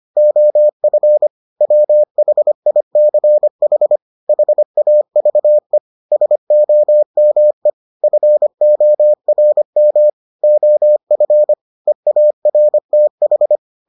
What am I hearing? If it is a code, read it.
Morse code, heard as a keyed tone, about 25 wpm, 600 Hz. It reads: OF WHICH HAVE SOME FORM OF EARTH